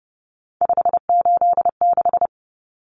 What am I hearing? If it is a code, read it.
Morse code, heard as a keyed tone, about 30 wpm, 700 Hz. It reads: 586
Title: Numbers - 3-Digit 30wpm